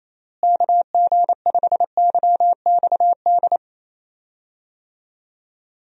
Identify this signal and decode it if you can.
Morse code, heard as a keyed tone, about 28 wpm, 700 Hz. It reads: KG5YXD
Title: USA Call Signs - 6 Characters - Repeated 3x - 28wpm